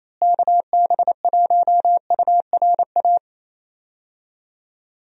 Morse code, heard as a keyed tone, about 28 wpm, 700 Hz. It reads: KB1URA
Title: USA Call Signs - 6 Characters - Repeated 3x - 28wpm